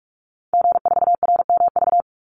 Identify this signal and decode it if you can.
Morse code, heard as a keyed tone, about 44 wpm, 700 Hz. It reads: G4RMV
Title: Call Signs Repeated 3x - 45wpm